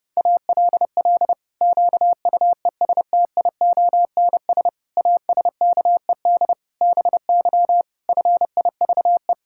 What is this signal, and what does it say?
Morse code, heard as a keyed tone, about 30 wpm, 700 Hz. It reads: ALL QUESTIONS ASKED BY FIVE
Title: Pangram Challenge v2 - 30wpm